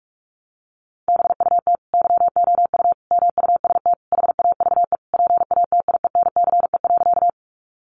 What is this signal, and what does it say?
Morse code, heard as a keyed tone, about 44 wpm, 700 Hz. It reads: BUT YOU MUST HAVE PATIENCE.
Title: Sentences from Top 1300 Words 45wpm